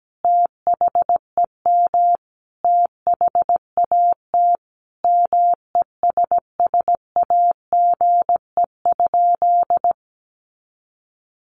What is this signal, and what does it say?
Morse code, heard as a keyed tone, about 17 wpm, 700 Hz. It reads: THEM THAT MESSAGE?
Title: Sentences from Top 1100 Words 17wpm